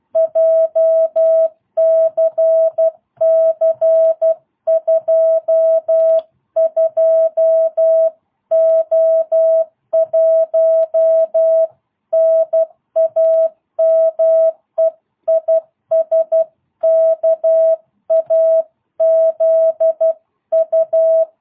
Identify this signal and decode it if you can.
Morse code, heard as a keyed tone, about 12 wpm, 650 Hz. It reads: JCC22O1 NAME IS KAZU